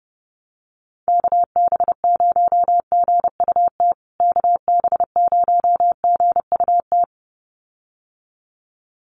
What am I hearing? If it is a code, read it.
Morse code, heard as a keyed tone, about 30 wpm, 700 Hz. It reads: KB0GUT KB0GUT
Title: USA Call Signs Repeated 2x 30wpm